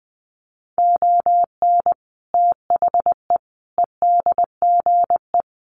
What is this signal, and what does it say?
Morse code, heard as a keyed tone, about 20 wpm, 700 Hz. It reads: ON THE EDGE
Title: Sets of 5 Words Encore2 20wpm